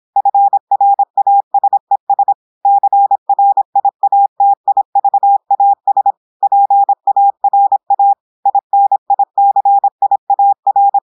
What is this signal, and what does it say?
Morse code, heard as a keyed tone, about 26 wpm, 800 Hz. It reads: FRASES CRIATIVAS PARA INICIAR
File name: portugais texte en clair 2 vitesse 26 mots minutes